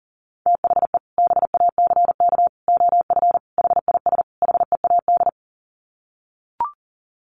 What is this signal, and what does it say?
Morse code, heard as a keyed tone, about 38 wpm, 700 Hz. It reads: THE BACK OF HIS HEAD
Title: Sets of 5 Words Encore 40wpm